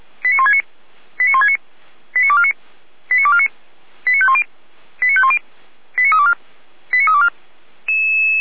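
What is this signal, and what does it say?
Five-tone selective call (ZVEI), heard as quick runs of five tones, tones of about 70 ms. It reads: 89169 89169 89249 89249 89510 89510 89324 89324
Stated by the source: ZVEI five-tone selective call received by SDR.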